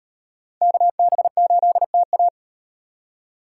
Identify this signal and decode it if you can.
Morse code, heard as a keyed tone, about 38 wpm, 700 Hz. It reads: KB8TA